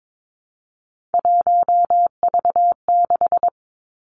Morse code, heard as a keyed tone, about 22 wpm, 700 Hz. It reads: 1V6
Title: Three Challenging Letter-Number Combinations 22wpm